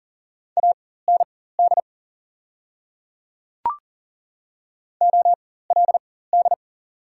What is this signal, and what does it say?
Morse code, heard as a keyed tone, about 40 wpm, spaced as for 19 wpm, 700 Hz. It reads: AND OLD
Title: 4x Character Spacing - Top 200 Words - 40wpm